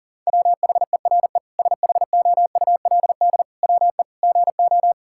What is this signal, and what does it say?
Morse code, heard as a keyed tone, about 37 wpm, 700 Hz. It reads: WHERE SHOULD WE GO
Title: Sentences from Top 1000 Words - Review Part 5 - 40wpm